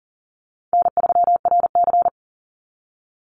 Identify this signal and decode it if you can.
Morse code, heard as a keyed tone, about 38 wpm, 700 Hz. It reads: N3RC
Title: Call Signs Repeated 2x - 40wpm